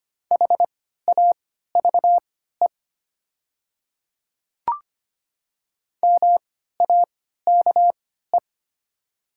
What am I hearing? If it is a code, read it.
Morse code, heard as a keyed tone, about 25 wpm, spaced as for 14 wpm, 700 Hz. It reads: HAVE MAKE
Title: 3x Character Spacing - Top 300 Words - 25wpm